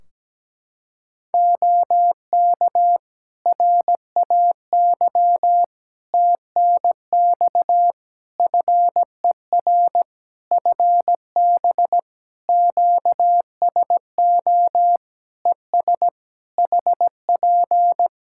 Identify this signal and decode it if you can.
Morse code, heard as a keyed tone, about 17 wpm, 700 Hz. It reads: OK RAY TNX FER FB QSO ES HP
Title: Practice QSO - Exchange 4 - Ending - 17wpm